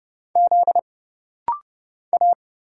Morse code, heard as a keyed tone, about 30 wpm, 700 Hz. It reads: Z A
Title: Single Letters - Rapid-Fire 30wpm